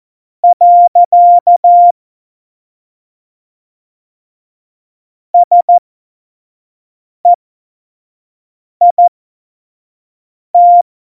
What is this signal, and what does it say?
Morse code, heard as a keyed tone, about 14 wpm, spaced as for 5 wpm, 700 Hz. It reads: . SEIT
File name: Morsekurs / Witze / Witz 13. Februar 2018, 25 F70